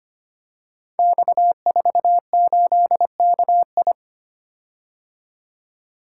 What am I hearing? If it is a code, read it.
Morse code, heard as a keyed tone, about 25 wpm, 700 Hz. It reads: X48KI
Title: Code Groups - 5 Letter:Numbers - 25wpm